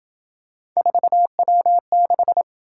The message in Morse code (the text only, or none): 4W6